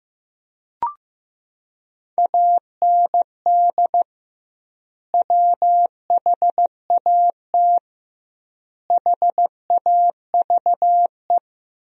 Morse code, heard as a keyed tone, about 15 wpm, 700 Hz. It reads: AND WHAT HAVE
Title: 2x Word-Spacing - Sentences from Top 500 Words - 15wpm